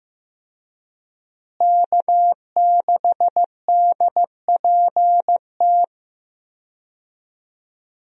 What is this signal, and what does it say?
Morse code, heard as a keyed tone, about 15 wpm, 700 Hz. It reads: K6DPT
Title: USA Call Signs - 5 Characters - Repeated 2x - 15wpm